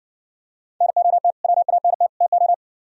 Morse code, heard as a keyed tone, about 55 wpm, 700 Hz. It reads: NOT WANT TO